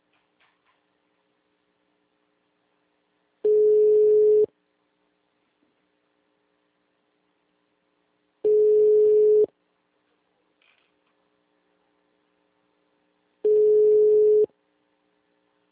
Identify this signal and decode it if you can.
Ringback tone (Europe, 425 Hz).